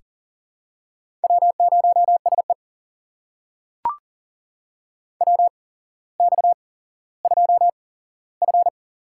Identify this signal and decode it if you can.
Morse code, heard as a keyed tone, about 37 wpm, spaced as for 17 wpm, 700 Hz. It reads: W0SE WX2F
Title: USA Call Signs - 4 Characters - 8x Character Spacing - 40wpm